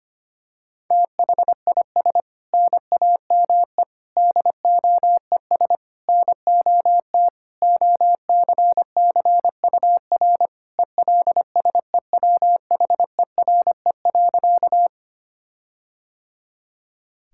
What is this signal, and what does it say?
Morse code, heard as a keyed tone, about 25 wpm, 700 Hz. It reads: THIS NAME DOES NOT OCCUR ELSEWHERE.
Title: Sentences from Top 1600 Words 25wpm